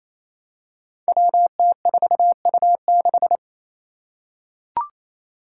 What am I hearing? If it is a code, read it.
Morse code, heard as a keyed tone, about 28 wpm, 700 Hz. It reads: WT4U6
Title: Five Challenging Letter-Number Combinations 28wpm